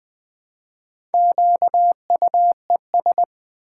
Morse code, heard as a keyed tone, about 20 wpm, 700 Hz. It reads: QUES
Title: Build A Word - 2 Letters At A Time - 8-Letter Words - 20wpm